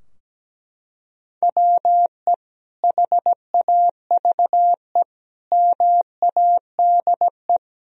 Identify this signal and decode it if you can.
Morse code, heard as a keyed tone, about 17 wpm, 700 Hz. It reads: WE HAVE MADE
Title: Sentences from Top 1000 Words - Review Part 3 - 17wpm